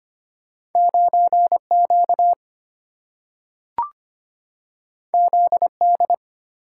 Morse code, heard as a keyed tone, about 25 wpm, 700 Hz. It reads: 9 Q Z D
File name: Two Letter-Number Combinations - 25wpm